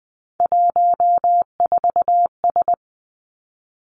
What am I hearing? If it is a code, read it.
Morse code, heard as a keyed tone, about 20 wpm, 700 Hz. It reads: 14S
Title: Three Challenging Letter-Number Combinations 20wpm